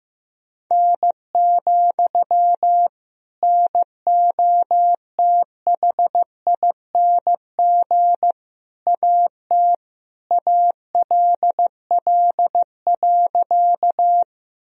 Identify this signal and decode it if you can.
Morse code, heard as a keyed tone, about 15 wpm, 700 Hz. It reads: N, NOTHING AT ALL.
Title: Sentences from Top 500 Words and Abbreviations 15wpm